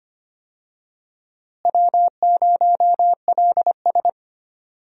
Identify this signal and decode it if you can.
Morse code, heard as a keyed tone, about 25 wpm, 700 Hz. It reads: W0LS
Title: USA Call Signs - 4 Characters - Repeated 2x - 25wpm